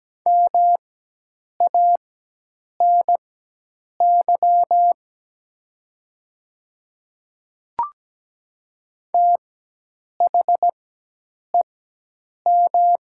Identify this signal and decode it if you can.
Morse code, heard as a keyed tone, about 17 wpm, spaced as for 8 wpm, 700 Hz. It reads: MANY THEM
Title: 4x Character Spacing - Top 100 Words - 17wpm